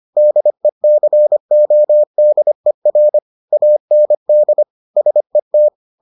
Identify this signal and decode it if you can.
Morse code, heard as a keyed tone, about 25 wpm, 600 Hz. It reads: DECODER AND SET